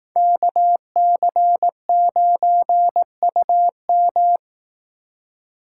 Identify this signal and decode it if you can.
Morse code, heard as a keyed tone, about 18 wpm, 700 Hz. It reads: KC9UM
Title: Speed Racing - USA 2x2 Call Signs 15wpm